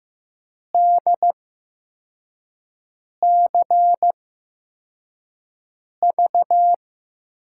Morse code, heard as a keyed tone, about 15 wpm, spaced as for 4 wpm, 700 Hz. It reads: DCV
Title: USA Call Signs - 5 Characters - 8x Character Spacing - 15wpm